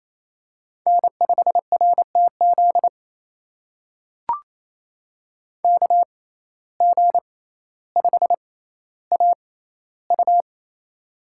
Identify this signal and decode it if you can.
Morse code, heard as a keyed tone, about 28 wpm, spaced as for 14 wpm, 700 Hz. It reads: N5RTZ KG5AU
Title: USA Call Signs - 5 Characters - 6x Character Spacing - 28wpm